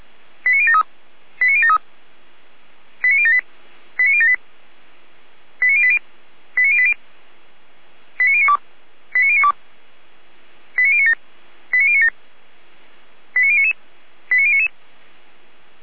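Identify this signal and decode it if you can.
Five-tone selective call (ZVEI), heard as quick runs of five tones, tones of about 70 ms. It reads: 89073 89073 89078 89078 89080 89080 89092 89092 89097 89097 89099 89099
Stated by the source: ZVEI five-tone selective call received by SDR, recorded by Feuerwehr Orlamünde (fire brigade).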